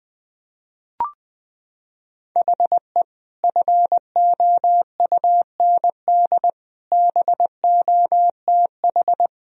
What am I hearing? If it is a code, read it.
Morse code, heard as a keyed tone, about 21 wpm, 700 Hz. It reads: HE FOUND BOTH
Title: Sentences from Top 1000 Words - Review Part 1 - 20wpm